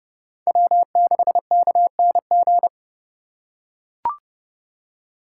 Morse code, heard as a keyed tone, about 30 wpm, 700 Hz. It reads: W6KNG
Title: USA Call Signs - 5 Characters - 2x Character Spacing - 30wpm